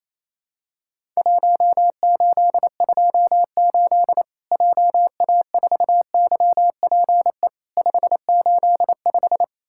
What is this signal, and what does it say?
Morse code, heard as a keyed tone, about 28 wpm, 700 Hz. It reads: 1828 JA4YPE 585
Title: Worked All Europe DX Contest - QTC Exchange - 28wpm